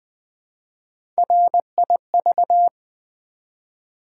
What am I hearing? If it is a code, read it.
Morse code, heard as a keyed tone, about 20 wpm, 700 Hz. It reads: RIV
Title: Build A Word - 1 Letter At A Time - 3 to 5-Letter Words - 20wpm